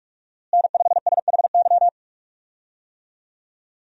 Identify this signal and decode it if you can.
Morse code, heard as a keyed tone, about 42 wpm, 700 Hz. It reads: N5SHY
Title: USA Call Signs - 5 Characters - Repeated 2x - 45wpm